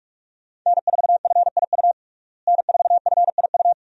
Morse code, heard as a keyed tone, about 46 wpm, 700 Hz. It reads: N4FIU N4FIU
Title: USA Call Signs - 5 Characters - Repeated 2x - 45wpm